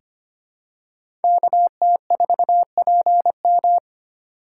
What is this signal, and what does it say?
Morse code, heard as a keyed tone, about 25 wpm, 700 Hz. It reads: KT4PM
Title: USA Call Signs - 2x Character Spacing - 25wpm